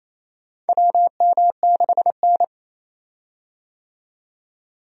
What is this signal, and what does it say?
Morse code, heard as a keyed tone, about 28 wpm, 700 Hz. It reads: WM6N
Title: USA Call Signs - 4 Characters - 28wpm